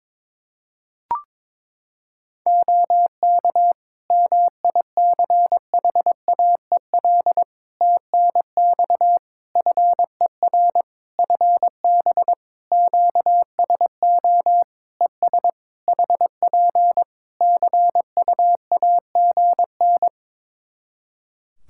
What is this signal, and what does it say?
Morse code, heard as a keyed tone, about 22 wpm, 700 Hz. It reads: OK MICHAEL TNX FER FB QSO ES HP CUAGN
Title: Practice QSO - Exchange 4 - Ending - 22wpm